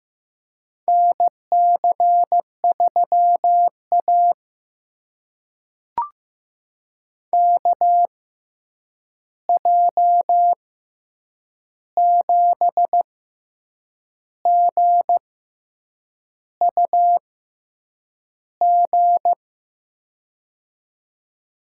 Morse code, heard as a keyed tone, about 15 wpm, spaced as for 7 wpm, 700 Hz. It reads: NC3A KJ7GUG
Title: USA Call Signs - 6x Character Spacing - 15wpm